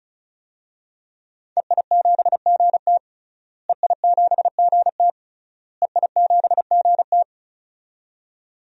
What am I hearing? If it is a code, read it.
Morse code, heard as a keyed tone, about 34 wpm, 700 Hz. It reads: EI7GT EI7GT EI7GT
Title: DXCC-100 Call Signs - 5 Characters - Repeated 3x - 35wpm